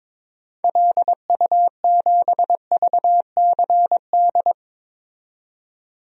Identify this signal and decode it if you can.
Morse code, heard as a keyed tone, about 22 wpm, 700 Hz. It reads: LU7VCD
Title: DXCC-100 Call Signs - 6 Characters - 22wpm